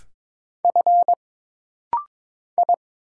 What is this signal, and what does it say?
Morse code, heard as a keyed tone, about 22 wpm, 700 Hz. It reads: F I
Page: This rapid-fire timing is geared to push your ICR (Instant Character Recognition) skills.